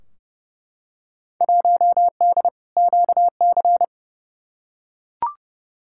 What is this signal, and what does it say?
Morse code, heard as a keyed tone, about 29 wpm, 700 Hz. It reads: 1D QC
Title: Field Day Exchanges Encore - 30wpm